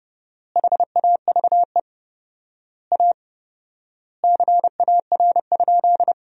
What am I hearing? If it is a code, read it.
Morse code, heard as a keyed tone, about 30 wpm, 700 Hz. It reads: HAVE A CAR?
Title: 4x Word-Spacing - Sentences from Top 300 Words - 30wpm